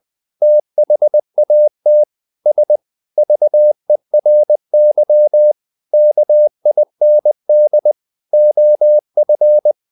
Morse code, heard as a keyed tone, about 20 wpm, 600 Hz. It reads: THAT S VERY KIND OF
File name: the old man and the sea / the old man and the sea part 1 20wpm